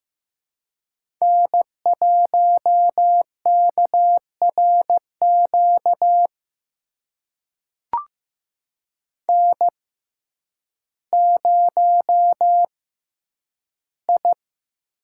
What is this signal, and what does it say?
Morse code, heard as a keyed tone, about 15 wpm, spaced as for 9 wpm, 700 Hz. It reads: N1KRQ N0I
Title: USA Call Signs - 5 Characters - 6x Character Spacing - 15wpm